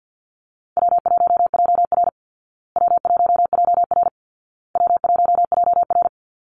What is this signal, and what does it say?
Morse code, heard as a keyed tone, about 50 wpm, 700 Hz. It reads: W1JR W1JR W1JR
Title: Easy Call Signs Repeated 3x - 50wpm